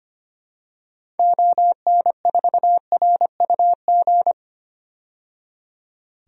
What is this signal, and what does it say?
Morse code, heard as a keyed tone, about 25 wpm, 700 Hz. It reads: ON4RUG
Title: DXCC-100 Call Signs - 6 Characters - 25wpm